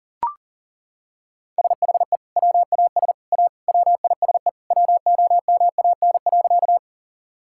Morse code, heard as a keyed tone, about 37 wpm, 700 Hz. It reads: SHE WAS A WISE WOMAN.
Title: Sentences from Top 1600 Words 40wpm